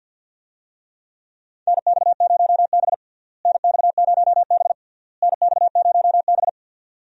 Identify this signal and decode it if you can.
Morse code, heard as a keyed tone, about 48 wpm, 700 Hz. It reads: NX0B NX0B NX0B
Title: USA Call Signs - 4 Characters - Repeated 3x - 50wpm